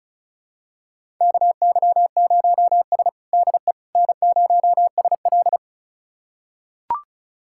Morse code, heard as a keyed tone, about 36 wpm, 700 Hz. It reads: KY0S DE N0SL
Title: USA Callsign DE Callsign - 4 Characters - 35wpm